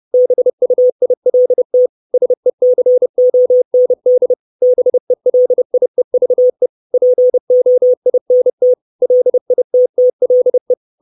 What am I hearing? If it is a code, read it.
Morse code, heard as a keyed tone, about 30 wpm, 500 Hz. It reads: BUILT SECOND BELIEVE POINT LITTLE
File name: 632 Words 30WPM